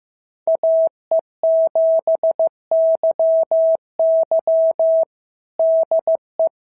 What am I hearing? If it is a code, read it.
Morse code, heard as a keyed tone, about 14 wpm, 650 Hz. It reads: AE7YY DE
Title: SOTA QSO Practice 15wpm